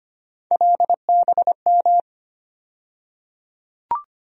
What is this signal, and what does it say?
Morse code, heard as a keyed tone, about 25 wpm, 700 Hz. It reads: LBM
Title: USA Call Sign Postfixes - 3 Letters 25wpm